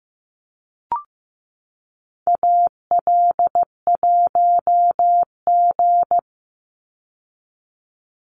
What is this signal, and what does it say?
Morse code, heard as a keyed tone, about 15 wpm, 700 Hz. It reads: AL1G
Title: Call Signs Repeated 2x - 15wpm